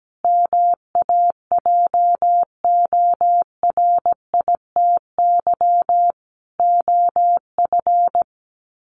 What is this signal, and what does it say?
Morse code, heard as a keyed tone, about 17 wpm, 700 Hz. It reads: MAJORITY OF
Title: Sets of 3 Words 17wpm